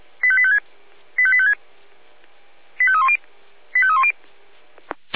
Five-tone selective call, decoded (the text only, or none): 85856 85856 85310 85310